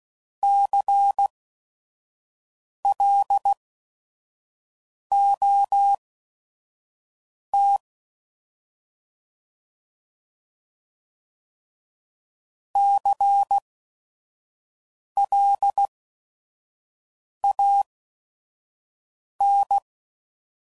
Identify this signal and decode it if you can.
Morse code, heard as a keyed tone, about 16 wpm, spaced as for 5 wpm, 800 Hz. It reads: CLOT CLAN
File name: Prefixes / By First Letter / Cs